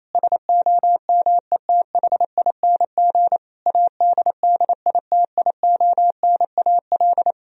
Morse code, heard as a keyed tone, about 28 wpm, 700 Hz. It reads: SOMETHING ADDITIONAL
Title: Sentences from Top 1000 Words - Review Part 4 - 28wpm